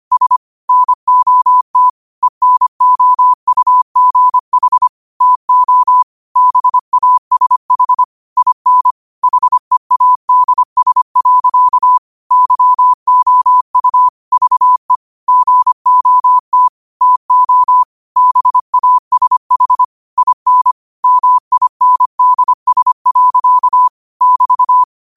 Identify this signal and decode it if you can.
Morse code, heard as a keyed tone, about 25 wpm, 1000 Hz. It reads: S NOT ENOUGH TO BASH IN HEADS. YOU VE GOT TO BASH IN MINDS. <BT>
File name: qotdfiles / 25wpm